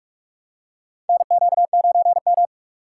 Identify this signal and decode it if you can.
Morse code, heard as a keyed tone, about 44 wpm, 700 Hz. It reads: NQ9M
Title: USA Call Signs - 3x Character Spacing - 45wpm